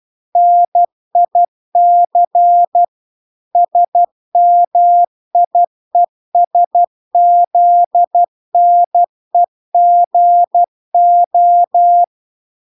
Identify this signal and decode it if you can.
Morse code, heard as a keyed tone, about 12 wpm, 700 Hz. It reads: NIC SMIESZNEGO